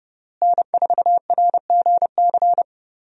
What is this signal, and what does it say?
Morse code, heard as a keyed tone, about 30 wpm, 700 Hz. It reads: N4RGC
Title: USA Call Signs - 5 Characters - Repeated 3x - 30wpm